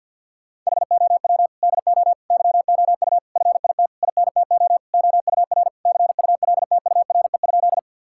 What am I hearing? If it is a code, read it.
Morse code, heard as a keyed tone, about 51 wpm, 700 Hz. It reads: HOW DO YOU FIT INTO OUR CULTURE?
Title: Sentences from Top 1000 Words - Review Part 6 - 50wpm